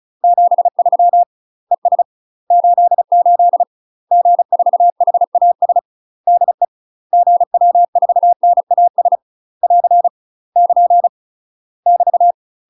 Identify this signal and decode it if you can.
Morse code, heard as a keyed tone, about 36 wpm, 700 Hz. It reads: 73 ES 88 G4HAS DE GW4NAS <AR> <KN> <BT>
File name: UK MASTER QSOs TRAINING / 35WPM / UK MASTER QSOs 0018